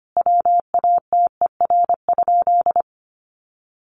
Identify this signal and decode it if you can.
Morse code, heard as a keyed tone, about 25 wpm, 700 Hz. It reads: WATER?
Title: Sentences from Top 1500 Words 25wpm